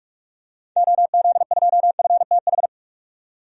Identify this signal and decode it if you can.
Morse code, heard as a keyed tone, about 42 wpm, 700 Hz. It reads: OZ2FTH
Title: DXCC-100 Call Signs - 6 Characters - 45wpm